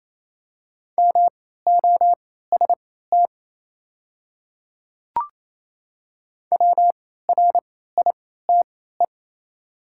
Morse code, heard as a keyed tone, about 27 wpm, spaced as for 16 wpm, 700 Hz. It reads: MOST WRITE
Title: 3x Character Spacing - Top 100 Words - 28wpm